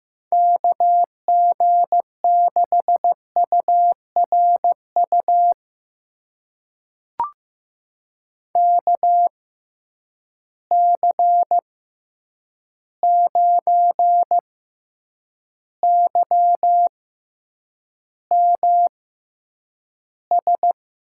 Morse code, heard as a keyed tone, about 15 wpm, spaced as for 8 wpm, 700 Hz. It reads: KG6URU KC9YMS
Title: USA Call Signs - 6 Characters - 6x Character Spacing - 15wpm